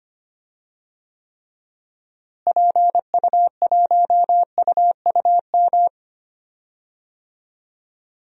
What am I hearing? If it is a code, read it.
Morse code, heard as a keyed tone, about 25 wpm, 700 Hz. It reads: PU1UUM
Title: DXCC-100 Call Signs - 6 Characters - Repeated 2x - 25wpm